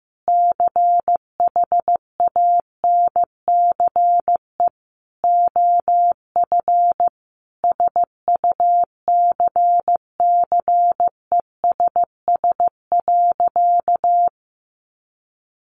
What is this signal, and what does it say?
Morse code, heard as a keyed tone, about 15 wpm, 700 Hz. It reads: CHANCE OF SUCCESS.
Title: Sentences from Top 1300 Words 15wpm